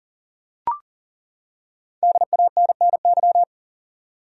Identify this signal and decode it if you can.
Morse code, heard as a keyed tone, about 37 wpm, 700 Hz. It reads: DANNY
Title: USA Common Names - Max 5 Letters - 40wpm